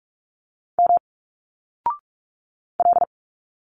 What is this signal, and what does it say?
Morse code, heard as a keyed tone, about 44 wpm, 700 Hz. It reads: M L
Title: Single Letters 45wpm